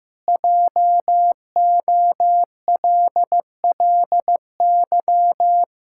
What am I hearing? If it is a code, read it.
Morse code, heard as a keyed tone, about 16 wpm, 700 Hz. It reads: JOLLY
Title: Pangram Challenge v2 - 15wpm